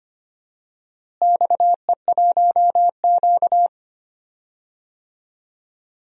Morse code, heard as a keyed tone, about 25 wpm, 700 Hz. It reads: XE1Q
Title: DXCC-100 Call Signs - 4 Characters - Repeated 2x - 25wpm